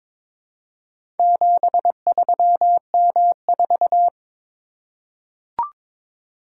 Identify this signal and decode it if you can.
Morse code, heard as a keyed tone, about 22 wpm, 700 Hz. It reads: 73M4
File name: practice / 22wpm / Four Letter-Number Combinations - 22wpm